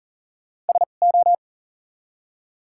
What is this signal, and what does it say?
Morse code, heard as a keyed tone, about 40 wpm, 700 Hz. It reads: S O
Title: 2x Character Spacing - Top 100 Words - 40wpm